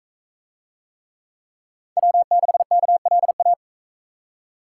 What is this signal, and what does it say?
Morse code, heard as a keyed tone, about 39 wpm, 700 Hz. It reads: W6KLA